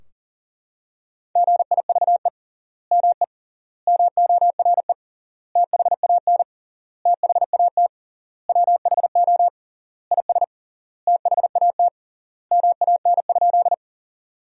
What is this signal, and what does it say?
Morse code, heard as a keyed tone, about 40 wpm, 700 Hz. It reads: GIVE ME MORE THAN THAT WHO IS THAT MAN?
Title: 3x Word-Spacing - Sentences from Top 200 Words - 40wpm